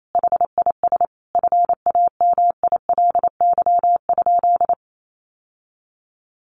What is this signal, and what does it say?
Morse code, heard as a keyed tone, about 28 wpm, 700 Hz. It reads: HIS FAMILY?
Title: Sentences from Top 1200 Words 28wpm